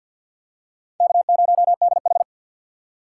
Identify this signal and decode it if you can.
Morse code, heard as a keyed tone, about 52 wpm, 700 Hz. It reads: K0DH
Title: USA Call Signs - 4 Characters - 3x Character Spacing - 50wpm